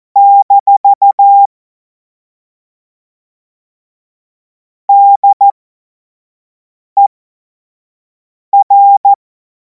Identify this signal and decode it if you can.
Morse code, heard as a keyed tone, about 14 wpm, spaced as for 5 wpm, 800 Hz. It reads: - DER
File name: Morsekurs / Witze / Witz 19. Mai 2026 25 F70